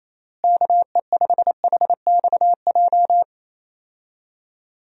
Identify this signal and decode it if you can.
Morse code, heard as a keyed tone, about 27 wpm, 700 Hz. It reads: KE5HXJ
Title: USA Call Signs - 6 Characters - Repeated 2x - 28wpm